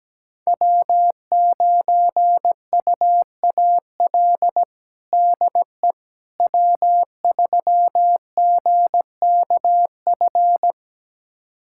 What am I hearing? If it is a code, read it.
Morse code, heard as a keyed tone, about 17 wpm, 700 Hz. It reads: W9UAL DE W3GKF
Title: USA Callsign DE Callsign - 5 Characters - 17wpm